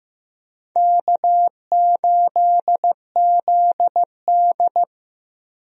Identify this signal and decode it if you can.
Morse code, heard as a keyed tone, about 15 wpm, 700 Hz. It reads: K8ZD
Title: USA Call Signs - 4 Characters - 15wpm